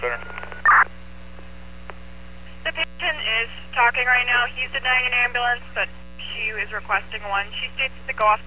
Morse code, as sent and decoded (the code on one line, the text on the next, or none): none